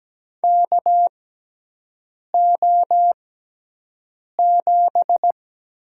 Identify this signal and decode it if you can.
Morse code, heard as a keyed tone, about 17 wpm, spaced as for 6 wpm, 700 Hz. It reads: KO7
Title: USA Call Signs - 4 Characters - 6x Character Spacing - 17wpm